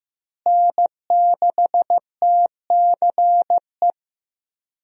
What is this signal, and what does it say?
Morse code, heard as a keyed tone, about 15 wpm, 700 Hz. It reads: N6TCE
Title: USA Call Signs - 5 Characters - Repeated 3x - 15wpm